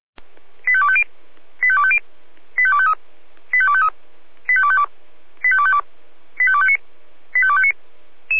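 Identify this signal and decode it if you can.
Five-tone selective call (ZVEI), heard as quick runs of five tones, tones of about 70 ms. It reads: 85250 85250 85253 85253 85252 85252 85259 85259
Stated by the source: Received on an SDR; ZVEI five-tone selective call.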